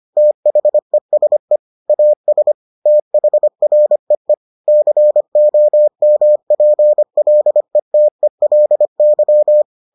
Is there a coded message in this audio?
Yes: Morse code from 0.2 to 9.6 s, heard as a keyed tone, about 25 wpm, 600 Hz. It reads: THESE AS THREE COMPLETELY